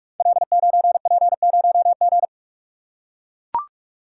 Morse code, heard as a keyed tone, about 44 wpm, 700 Hz. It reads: L9P0G
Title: Code Groups - 5 Letter:Numbers - 45wpm